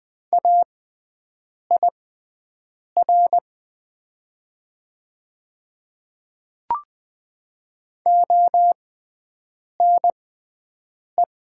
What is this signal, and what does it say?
Morse code, heard as a keyed tone, about 20 wpm, spaced as for 7 wpm, 700 Hz. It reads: AIR ONE
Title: 6x Character Spacing - Top 200 Words - 20wpm